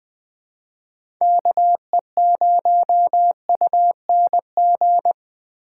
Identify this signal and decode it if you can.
Morse code, heard as a keyed tone, about 20 wpm, 700 Hz. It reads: KE0UNG
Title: USA Call Signs - 6 Characters - 2x Character Spacing - 20wpm